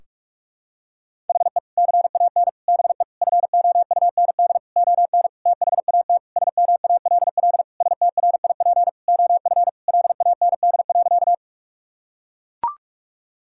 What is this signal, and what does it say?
Morse code, heard as a keyed tone, about 41 wpm, 700 Hz. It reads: HE CAN BE FOUND ON THAT SMALL STRIP OF LAND.
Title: Sentences from Top 1600 Words 45wpm